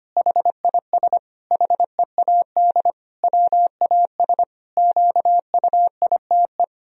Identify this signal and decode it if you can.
Morse code, heard as a keyed tone, about 25 wpm, 700 Hz. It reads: HIS HEAD WAS QUITE
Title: Sentences from Top 1000 Words - Review Part 2 - 25wpm